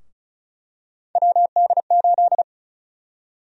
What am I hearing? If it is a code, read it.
Morse code, heard as a keyed tone, about 34 wpm, 700 Hz. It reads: WD8
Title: USA Call Sign Prefixes - 35wpm